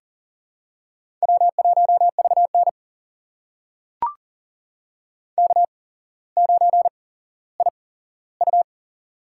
Morse code, heard as a keyed tone, about 43 wpm, spaced as for 17 wpm, 700 Hz. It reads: W1VN K9IU
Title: USA Call Signs - 4 Characters - 8x Character Spacing - 40wpm